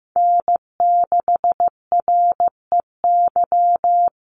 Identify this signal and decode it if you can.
Morse code, heard as a keyed tone, about 15 wpm, 700 Hz. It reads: N6REY
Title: USA Call Signs 15wpm